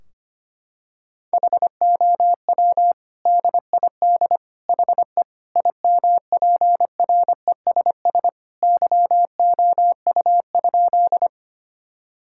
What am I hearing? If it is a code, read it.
Morse code, heard as a keyed tone, about 25 wpm, 700 Hz. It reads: HOW DID HE IMPRESS YOU?
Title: Sentences from Top 1600 Words 25wpm